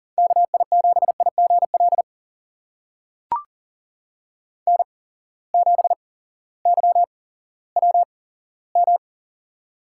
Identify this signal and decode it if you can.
Morse code, heard as a keyed tone, about 39 wpm, spaced as for 18 wpm, 700 Hz. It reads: KI7IGL N7YWM
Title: USA Call Signs - 8x Character Spacing - 40wpm